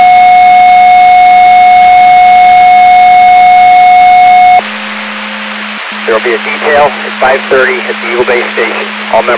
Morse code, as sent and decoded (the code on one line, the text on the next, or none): none